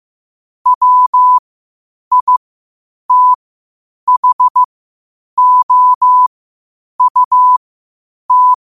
Morse code, heard as a keyed tone, about 15 wpm, spaced as for 8 wpm, 1000 Hz. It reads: WITHOUT
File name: qotdfiles / 8wpm